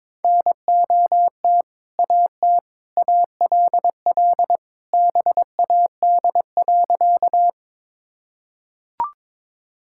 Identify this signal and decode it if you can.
Morse code, heard as a keyed tone, about 22 wpm, 700 Hz. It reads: NOT AT ALL BAD.
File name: practice / 22wpm / Sentences with 2-Word Phrases - 22wpm